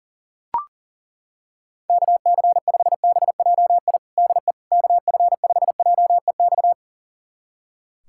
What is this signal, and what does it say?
Morse code, heard as a keyed tone, about 40 wpm, 700 Hz. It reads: KC5BJI DE KF5JEX
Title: USA Callsign DE Callsign - 6 Characters - 40wpm